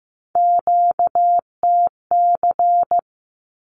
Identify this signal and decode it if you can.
Morse code, heard as a keyed tone, about 15 wpm, 700 Hz. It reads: QTC
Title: Q-Codes 15wpm